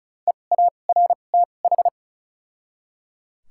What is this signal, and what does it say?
Morse code, heard as a keyed tone, about 36 wpm, 700 Hz. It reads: E A R T H
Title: 2x Character Spacing - Top 200 Words - 35wpm